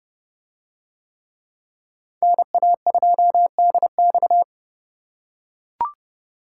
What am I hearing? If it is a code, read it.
Morse code, heard as a keyed tone, about 27 wpm, 700 Hz. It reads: NA2DX